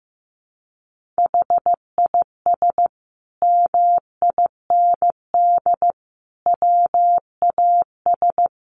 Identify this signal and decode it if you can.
Morse code, heard as a keyed tone, about 15 wpm, 700 Hz. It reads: HIS MIND WAS
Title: Sentences from Top 1500 Words 15wpm